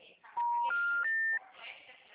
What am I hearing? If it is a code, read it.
Special information tone, with speech.